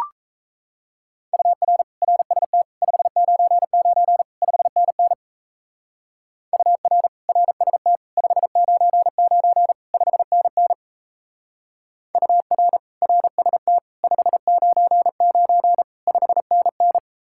Morse code, heard as a keyed tone, about 35 wpm, 700 Hz. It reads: UR RST 599 5NN UR RST 599 5NN UR RST 599 5NN
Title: Speed-Racing Common QSO Phrases - 28wpm